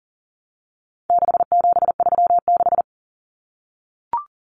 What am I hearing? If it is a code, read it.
Morse code, heard as a keyed tone, about 40 wpm, 700 Hz. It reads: 6736
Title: Common Cut Numbers - 4-Digit 40wpm